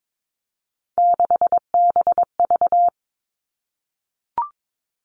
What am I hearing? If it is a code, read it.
Morse code, heard as a keyed tone, about 22 wpm, 700 Hz. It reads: 6BV
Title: Three Challenging Letter-Number Combinations 22wpm